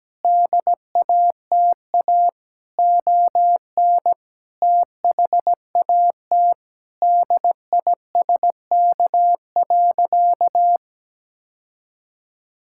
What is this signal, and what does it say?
Morse code, heard as a keyed tone, about 17 wpm, 700 Hz. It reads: DATA ON THAT DISK.
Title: Sentences from Top 1000 Words - Review Part 6 - 17wpm